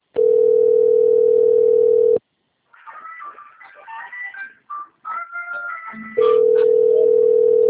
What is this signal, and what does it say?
Ringback tone (North America), 440 + 480 Hz.